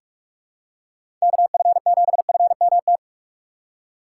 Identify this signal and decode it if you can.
Morse code, heard as a keyed tone, about 43 wpm, 700 Hz. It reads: KF7FMT
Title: USA Call Signs - 6 Characters - 3x Character Spacing - 45wpm